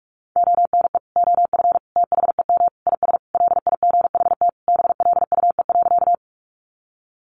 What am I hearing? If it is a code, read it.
Morse code, heard as a keyed tone, about 44 wpm, 700 Hz. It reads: ONE OF THEM IS LIGHT BLUE.
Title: Additional Easy Phrases 45wpm